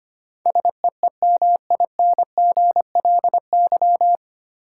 Morse code, heard as a keyed tone, about 25 wpm, 700 Hz. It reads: SEEMINGLY
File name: Nine Letter Words - 25wpm